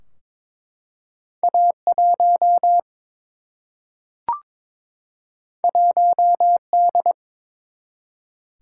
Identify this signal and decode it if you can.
Morse code, heard as a keyed tone, about 22 wpm, 700 Hz. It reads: A1 1D
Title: Single-Letter Single-Number 22wpm